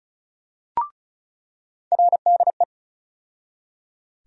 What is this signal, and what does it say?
Morse code, heard as a keyed tone, about 34 wpm, 700 Hz. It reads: RDE
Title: USA Call Sign Postfixes - 3 Letters 35wpm